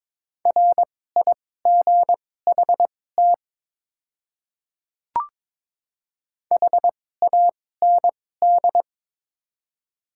Morse code, heard as a keyed tone, about 23 wpm, spaced as for 16 wpm, 700 Hz. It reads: RIGHT HAND
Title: 2x Character Spacing - Top 300 Words - 22wpm